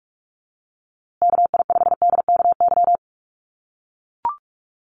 Morse code, heard as a keyed tone, about 44 wpm, 700 Hz. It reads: KI5DKY
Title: USA Call Signs Repeated 2x 45wpm